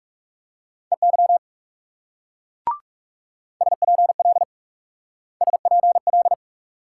Morse code, heard as a keyed tone, about 39 wpm, 700 Hz. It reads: EY SPL SPL